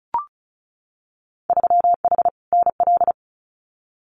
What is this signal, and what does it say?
Morse code, heard as a keyed tone, about 36 wpm, 700 Hz. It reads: 3H NL
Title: Winter Field Day Exchanges Encore 35wpm